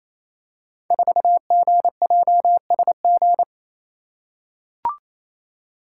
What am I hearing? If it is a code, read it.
Morse code, heard as a keyed tone, about 28 wpm, 700 Hz. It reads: 4GJSG
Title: Code Groups - 5 Letter:Numbers - 28wpm